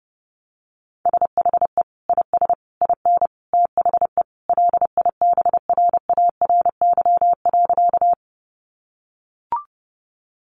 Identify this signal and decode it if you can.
Morse code, heard as a keyed tone, about 30 wpm, 700 Hz. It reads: SHE IS IN THE LIBRARY.
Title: Sentences from Top 600 Words 30wpm